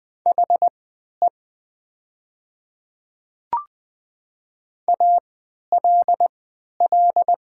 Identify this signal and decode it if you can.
Morse code, heard as a keyed tone, about 20 wpm, spaced as for 11 wpm, 700 Hz. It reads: HE ALL
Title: 3x Character Spacing - Top 100 Words - 20wpm